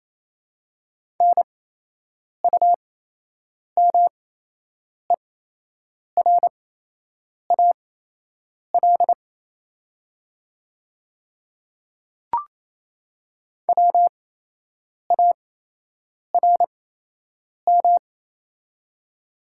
Morse code, heard as a keyed tone, about 28 wpm, spaced as for 8 wpm, 690 Hz. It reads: NUMERAL WARM
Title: 8x Character Spacing - Top 500 Words - 28wpm